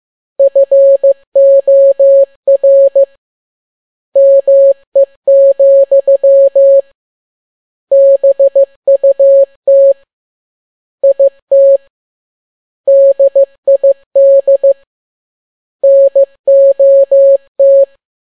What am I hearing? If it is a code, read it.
Morse code, heard as a keyed tone, about 15 wpm, 550 Hz. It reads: FOR ME, BUT IT DID NOT
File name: n6tt p1 15C13